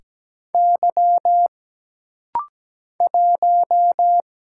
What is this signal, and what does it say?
Morse code, heard as a keyed tone, about 17 wpm, 700 Hz. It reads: Y 1
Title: Single Letter-Number Alt - Rapid-Fire 17wpm